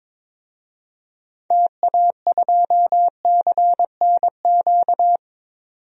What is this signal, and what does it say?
Morse code, heard as a keyed tone, about 22 wpm, 700 Hz. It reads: TA2CNQ
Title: DXCC-100 Call Signs - 6 Characters - 22wpm